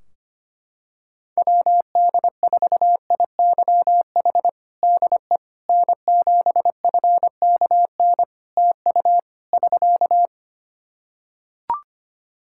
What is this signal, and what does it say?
Morse code, heard as a keyed tone, about 25 wpm, 700 Hz. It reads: WD4IYH DE N7FKN TU <SK>
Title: Practice QSO - Exchange 4 - Ending - 25wpm